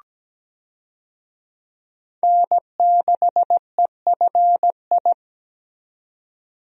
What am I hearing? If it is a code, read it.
Morse code, heard as a keyed tone, about 17 wpm, 700 Hz. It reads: N6EFI
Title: USA Call Signs - 5 Characters - 17wpm